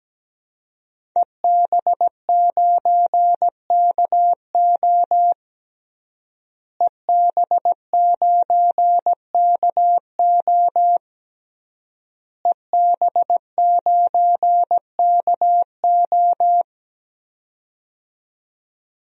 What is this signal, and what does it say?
Morse code, heard as a keyed tone, about 17 wpm, 700 Hz. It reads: EB9KO EB9KO EB9KO
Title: DXCC-100 Call Signs - 5 Characters - Repeated 3x - 17wpm